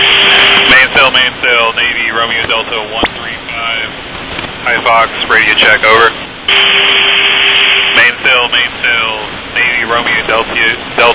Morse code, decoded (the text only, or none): none